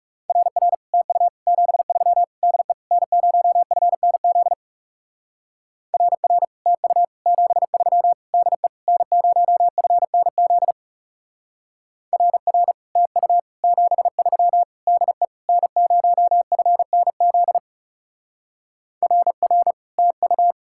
Morse code, heard as a keyed tone, about 35 wpm, 700 Hz. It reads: RR TU 73 DE N0FNZ RR TU 73 DE N0FNZ RR TU 73 DE N0FNZ RR TU
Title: Speed-Racing Common QSO Phrases - 30wpm